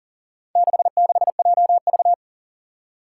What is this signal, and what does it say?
Morse code, heard as a keyed tone, about 40 wpm, 700 Hz. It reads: 66JV
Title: Four Challenging Letter-Number Combinations 40wpm